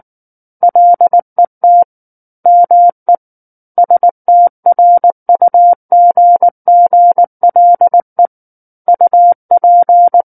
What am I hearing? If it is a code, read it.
Morse code, heard as a keyed tone, about 19 wpm, 700 Hz. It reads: LET ME STRUGGLE UP
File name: Book One / 10-Chapter 10 - 19 WPM